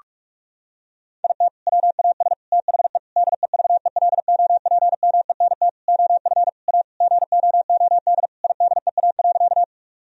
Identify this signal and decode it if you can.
Morse code, heard as a keyed tone, about 46 wpm, 700 Hz. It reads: IT WAS THE DEVELOPMENT OF A GOOD IDEA.
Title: Sentences from Top 1000 Words - Review Part 4 - 45wpm